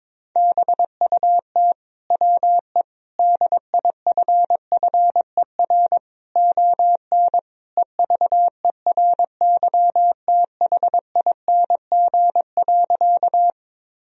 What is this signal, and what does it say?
Morse code, heard as a keyed tone, about 22 wpm, 700 Hz. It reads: BUT WE DIFFER ON EVERYTHING.
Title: Sentences from Top 1000 Words - Review Part 1 - 22wpm